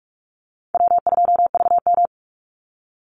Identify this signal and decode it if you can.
Morse code, heard as a keyed tone, about 44 wpm, 700 Hz. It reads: W2VM
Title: Easy Call Signs Repeated 3x - 45wpm